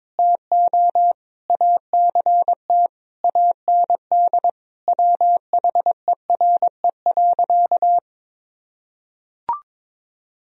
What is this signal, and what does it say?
Morse code, heard as a keyed tone, about 22 wpm, 700 Hz. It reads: TO ACT AND WHERE.
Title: Sentences from Top 1000 Words - Review Part 1 - 22wpm